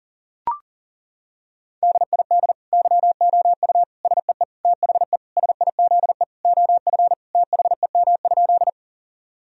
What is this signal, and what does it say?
Morse code, heard as a keyed tone, about 40 wpm, 700 Hz. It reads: DID YOU SEE THE SIZE OF THEM?
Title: Sentences from Top 1000 Words - Review Part 2 - 40wpm